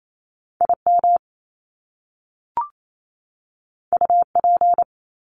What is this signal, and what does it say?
Morse code, heard as a keyed tone, about 28 wpm, 700 Hz. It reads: IM UP
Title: Top Two Letter Combinations in Words 28wpm